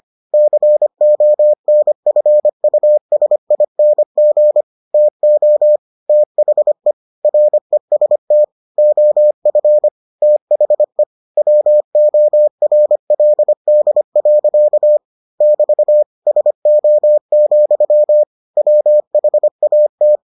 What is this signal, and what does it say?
Morse code, heard as a keyed tone, about 25 wpm, 600 Hz. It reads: CONFUSING TO THE REST OF THE WORLD. <BT> SO, WHAT